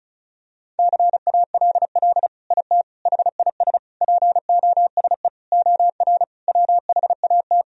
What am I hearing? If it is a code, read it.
Morse code, heard as a keyed tone, about 34 wpm, 700 Hz. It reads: CALL IT HIS POSE OR WHAT
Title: Sentences from Top 1000 Words - Review Part 2 - 35wpm